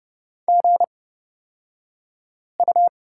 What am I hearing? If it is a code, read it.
Morse code, heard as a keyed tone, about 29 wpm, spaced as for 5 wpm, 700 Hz. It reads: GU
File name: Lesson 36d - BT - Two Characters 30wpm